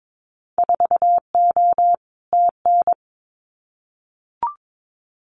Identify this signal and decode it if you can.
Morse code, heard as a keyed tone, about 22 wpm, 700 Hz. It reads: 4O TN
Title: Winter Field Day Exchanges Encore 22wpm